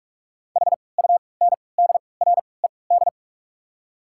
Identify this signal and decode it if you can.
Morse code, heard as a keyed tone, about 41 wpm, spaced as for 25 wpm, 700 Hz. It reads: HUNDRED
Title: 3x Character Spacing - Top 500 Words - 45wpm